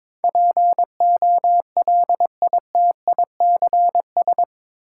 Morse code, heard as a keyed tone, about 22 wpm, 700 Hz. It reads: POLITICS
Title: Sentences from Top 1000 Words - New Word Drill - 22wpm